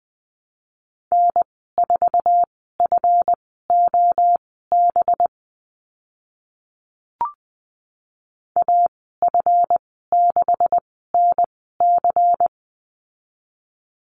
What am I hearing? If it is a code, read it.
Morse code, heard as a keyed tone, about 20 wpm, spaced as for 14 wpm, 700 Hz. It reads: N4FOB AF6NC
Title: 186 Lesson 30g - Letter b - Call Signs 20wpm